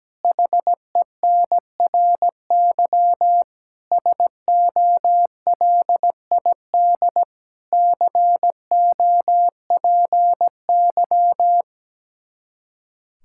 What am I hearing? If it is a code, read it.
Morse code, heard as a keyed tone, about 17 wpm, 700 Hz. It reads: HENRY SOLID COPY
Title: Practice QSO - Exchange 3 - More Info - 17wpm